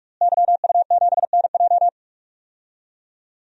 Morse code, heard as a keyed tone, about 41 wpm, 700 Hz. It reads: YU7NJ
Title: DXCC-100 Call Signs - 5 Characters - Repeated 2x - 45wpm